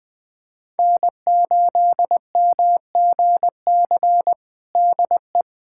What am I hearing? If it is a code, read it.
Morse code, heard as a keyed tone, about 19 wpm, 700 Hz. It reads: N8MGC DE
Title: USA Callsign DE Callsign 20wpm